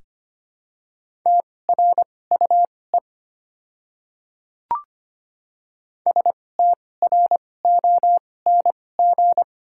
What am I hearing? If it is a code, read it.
Morse code, heard as a keyed tone, about 25 wpm, spaced as for 18 wpm, 690 Hz. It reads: TRUE STRONG
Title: 2x Character Spacing - Top 500 Words - 25wpm